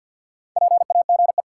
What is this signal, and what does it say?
Morse code, heard as a keyed tone, about 50 wpm, 700 Hz. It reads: PAGE
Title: Sets of 5 Words Encore - New Word Drill - 50wpm